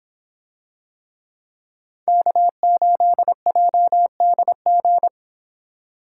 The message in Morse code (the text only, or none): K8JDG